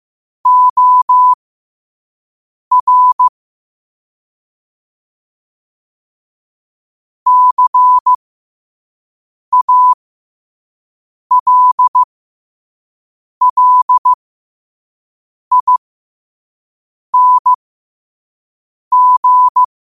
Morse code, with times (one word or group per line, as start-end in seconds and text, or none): 0.5-3.3 OR
7.3-19.6 CALLING